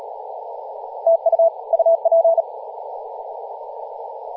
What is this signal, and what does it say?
Morse code, heard as a keyed tone, about 42 wpm, 660 Hz, with radio noise. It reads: TU UP
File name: posts / 2016-aug-26-2236z-yj0cow-18-073mhz-cw